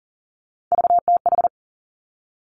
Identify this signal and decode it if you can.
Morse code, heard as a keyed tone, about 41 wpm, 700 Hz. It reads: VTH
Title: Three Challenging Letter-Number Combinations 40wpm